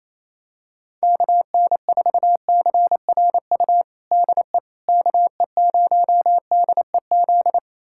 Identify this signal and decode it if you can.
Morse code, heard as a keyed tone, about 28 wpm, 700 Hz. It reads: KN4CRU DE KE0DEZ
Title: USA Callsign DE Callsign - 6 Characters - 28wpm